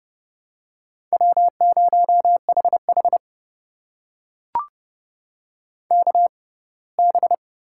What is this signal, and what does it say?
Morse code, heard as a keyed tone, about 30 wpm, spaced as for 20 wpm, 700 Hz. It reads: W0HH KB
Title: USA Call Signs - 6x Character Spacing - 30wpm